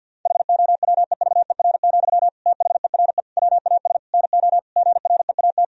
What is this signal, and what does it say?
Morse code, heard as a keyed tone, about 50 wpm, 700 Hz. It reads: HOWEVER, THERE WAS NO GREAT
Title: Sentences from Top 1000 Words - Review Part 4 - 50wpm